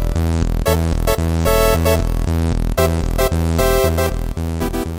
There is no code.